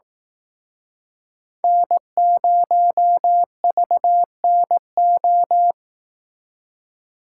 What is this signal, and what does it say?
Morse code, heard as a keyed tone, about 18 wpm, 700 Hz. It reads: N0VNO
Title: Speed Racing - USA 1x3 Call Signs 15wpm